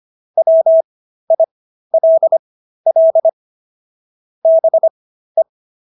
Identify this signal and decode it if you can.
Morse code, heard as a keyed tone, about 25 wpm, spaced as for 13 wpm, 650 Hz. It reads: WILL BE